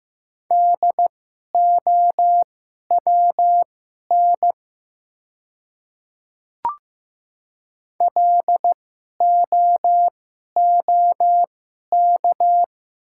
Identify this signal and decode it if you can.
Morse code, heard as a keyed tone, about 15 wpm, spaced as for 11 wpm, 700 Hz. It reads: DOWN LOOK
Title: 2x Character Spacing - Top 100 Words - 15wpm